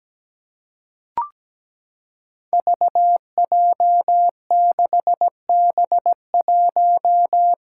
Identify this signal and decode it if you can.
Morse code, heard as a keyed tone, about 17 wpm, 700 Hz. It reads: VJ6B1
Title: Five Challenging Letter-Number Combinations 17wpm